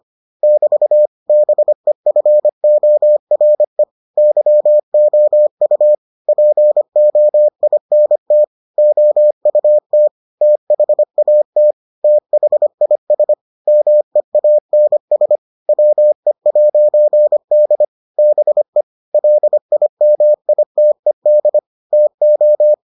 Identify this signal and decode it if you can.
Morse code, heard as a keyed tone, about 25 wpm, 600 Hz. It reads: <BT> BEFORE YOU POINT OUT THAT THIS MEANS WE'D BE LIMITED TO